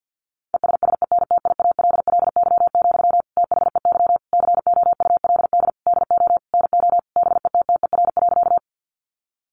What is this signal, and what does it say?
Morse code, heard as a keyed tone, about 50 wpm, 700 Hz. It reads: ESSENTIALLY, THEY COULD DO NO BETTER.
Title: Sentences from Top 1300 Words 50wpm